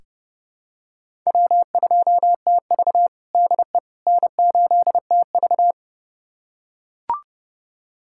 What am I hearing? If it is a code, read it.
Morse code, heard as a keyed tone, about 30 wpm, 700 Hz. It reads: W2TV DE N8TV
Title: USA Callsign DE Callsign - 4 Characters - 30wpm